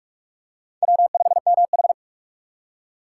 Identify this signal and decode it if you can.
Morse code, heard as a keyed tone, about 46 wpm, 700 Hz. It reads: W5MH
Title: USA Call Signs - 4 Characters - 3x Character Spacing - 45wpm